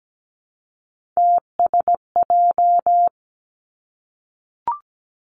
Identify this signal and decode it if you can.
Morse code, heard as a keyed tone, about 17 wpm, 700 Hz. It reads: TSJ
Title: Three Challenging Letter-Number Combinations 17wpm